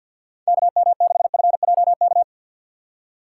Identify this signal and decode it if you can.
Morse code, heard as a keyed tone, about 53 wpm, 700 Hz. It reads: KM6FJK
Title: USA Call Signs - 6 Characters - 4x Character Spacing - 50wpm